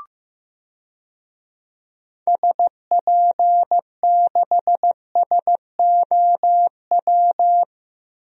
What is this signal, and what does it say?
Morse code, heard as a keyed tone, about 15 wpm, 700 Hz. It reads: SP6SOW
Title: DXCC-100 Call Signs - 6 Characters - 15wpm